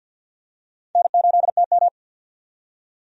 Morse code, heard as a keyed tone, about 46 wpm, 700 Hz. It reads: N8TM
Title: USA Call Signs - 4 Characters - 50wpm